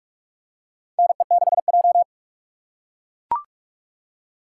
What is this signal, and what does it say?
Morse code, heard as a keyed tone, about 42 wpm, 700 Hz. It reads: NE6J
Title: USA Call Signs - 4 Characters - 4x Character Spacing - 45wpm